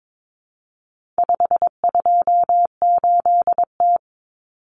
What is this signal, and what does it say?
Morse code, heard as a keyed tone, about 22 wpm, 700 Hz. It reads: 528T
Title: Common Cut Numbers - 4-Digit 22wpm